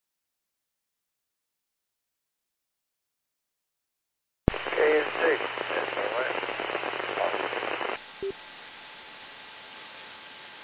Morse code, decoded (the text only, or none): none